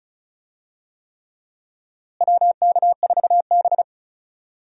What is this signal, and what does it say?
Morse code, heard as a keyed tone, about 34 wpm, 700 Hz. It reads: WK4B
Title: USA Call Signs - 4 Characters - Repeated 3x - 35wpm